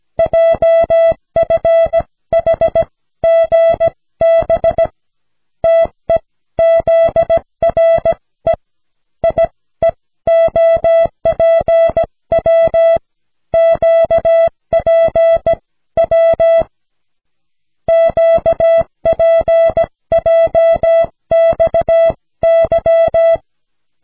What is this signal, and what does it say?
Morse code, heard as a keyed tone, about 16 wpm, spaced as for 14 wpm, 660 Hz. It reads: JFHGB TEZRE IEOPW QPW QPJXY
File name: cw / snelheid 4